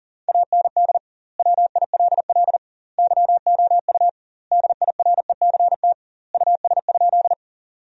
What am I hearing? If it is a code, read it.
Morse code, heard as a keyed tone, about 40 wpm, 700 Hz. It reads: AND WILL YOU DIRECT US?
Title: 2x Word-Spacing - Sentences from Top 400 Words - 40wpm